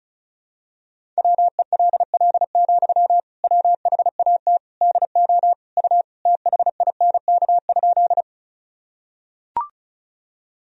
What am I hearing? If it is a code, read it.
Morse code, heard as a keyed tone, about 36 wpm, 700 Hz. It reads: WELL, WHAT DO U THINK?
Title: Sentences from Top 200 Words and Abbreviations 35wpm